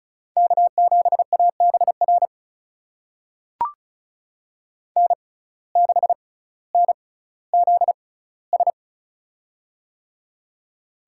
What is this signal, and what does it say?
Morse code, heard as a keyed tone, about 36 wpm, spaced as for 18 wpm, 700 Hz. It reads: K7ABR N6NZS
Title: USA Call Signs - 5 Characters - 6x Character Spacing - 35wpm